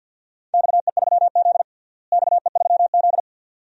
Morse code, heard as a keyed tone, about 46 wpm, 700 Hz. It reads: XE3Z XE3Z
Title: DXCC-100 Call Signs - 4 Characters - Repeated 2x - 50wpm